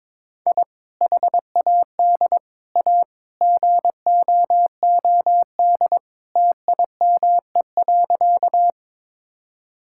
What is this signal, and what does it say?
Morse code, heard as a keyed tone, about 22 wpm, 700 Hz. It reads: I HAD A GOOD TIME.
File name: practice / 22wpm / Sentences with 3-Word Phrases - 22wpm